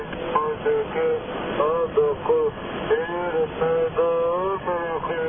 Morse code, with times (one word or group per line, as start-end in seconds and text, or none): none